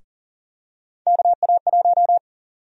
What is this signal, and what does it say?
Morse code, heard as a keyed tone, about 40 wpm, 700 Hz. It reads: KA1
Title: USA Call Sign Prefixes - Repeated 2x - 40wpm